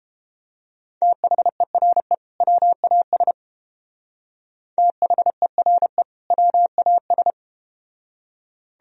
Morse code, heard as a keyed tone, about 31 wpm, 700 Hz. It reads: THERE WAS THERE WAS